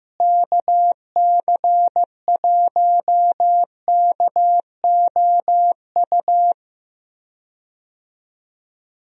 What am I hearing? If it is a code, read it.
Morse code, heard as a keyed tone, about 15 wpm, 700 Hz. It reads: KC1KOU
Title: USA Call Signs - 6 Characters - Repeated 3x - 15wpm